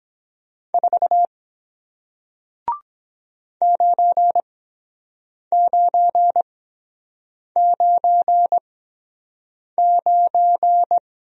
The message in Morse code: 4 9999

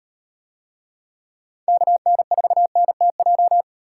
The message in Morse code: KN4NTJ

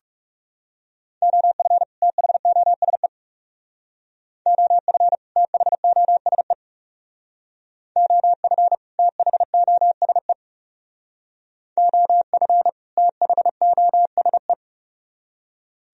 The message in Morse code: OF THOSE OF THOSE OF THOSE OF THOSE

